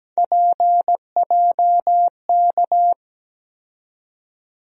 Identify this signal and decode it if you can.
Morse code, heard as a keyed tone, about 17 wpm, 700 Hz. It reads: PJK
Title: USA Call Sign Postfixes - 3 Letters 17wpm